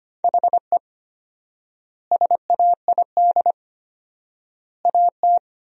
Morse code, heard as a keyed tone, about 25 wpm, 700 Hz. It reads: HE SAID AT
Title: 4x Word-Spacing - Sentences from Top 300 Words - 25wpm